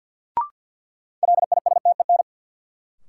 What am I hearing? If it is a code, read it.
Morse code, heard as a keyed tone, about 53 wpm, 700 Hz. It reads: LISTEN
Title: Sentences from Top 400 Words - New Word Drill - 50wpm